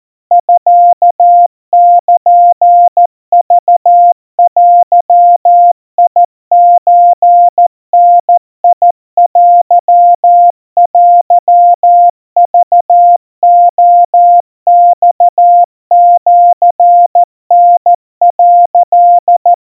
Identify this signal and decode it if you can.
Morse code (Japanese Wabun code), heard as a keyed tone, about 14 wpm, 700 Hz. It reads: ミルクデソダテテクレマシタ」
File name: kamyu 50 cpm 700 Hz Crispy